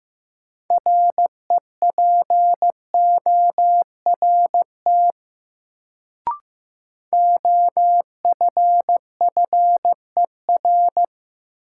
Morse code, heard as a keyed tone, about 15 wpm, 700 Hz. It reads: REPORT OFFER
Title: Sentences from Top 900 Words - New Word Drill - 15wpm